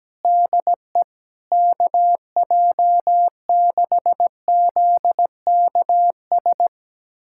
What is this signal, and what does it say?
Morse code, heard as a keyed tone, about 17 wpm, 700 Hz. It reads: DE KJ6ZKS
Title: USA Callsign DE Callsign - 6 Characters - 17wpm